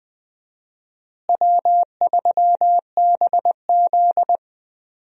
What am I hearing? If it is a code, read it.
Morse code, heard as a keyed tone, about 21 wpm, 700 Hz. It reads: W3BZ
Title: USA Call Signs - 4 Characters - Repeated 3x - 20wpm